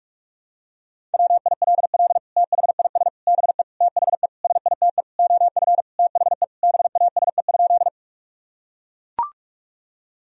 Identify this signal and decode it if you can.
Morse code, heard as a keyed tone, about 42 wpm, 700 Hz. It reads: WILL THIS BE THE SITE OF THE BASE?
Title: Sentences from Top 1000 Words - Review Part 1 - 45wpm